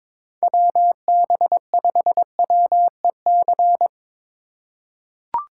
Morse code, heard as a keyed tone, about 22 wpm, 700 Hz. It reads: WB5WEC
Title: K1USN SST QSO Practice - 22wpm